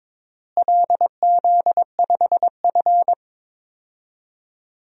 Morse code, heard as a keyed tone, about 22 wpm, 700 Hz. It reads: LZ5F
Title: DXCC-100 Call Signs - 4 Characters - Repeated 3x - 22wpm